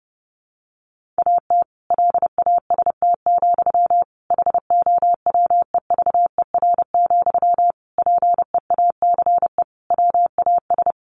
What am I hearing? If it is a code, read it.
Morse code, heard as a keyed tone, about 30 wpm, 700 Hz. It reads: AT LAST, HOWEVER, PEACE WAS
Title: Sentences from Top 1500 Words 30wpm